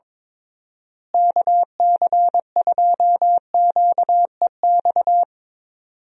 Morse code, heard as a keyed tone, about 21 wpm, 700 Hz. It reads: KC2QEX
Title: USA Call Signs - 6 Characters - Repeated 2x - 22wpm